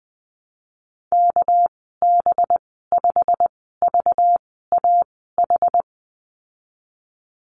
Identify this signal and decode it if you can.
Morse code, heard as a keyed tone, about 20 wpm, spaced as for 15 wpm, 700 Hz. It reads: KB5VAH
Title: 238 Lesson 38g - Letter q - Call Signs 20wpm